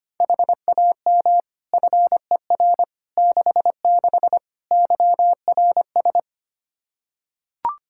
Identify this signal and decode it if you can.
Morse code, heard as a keyed tone, about 25 wpm, 700 Hz. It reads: HAM FER 66 YRS
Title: K1USN SST QSO Practice 25wpm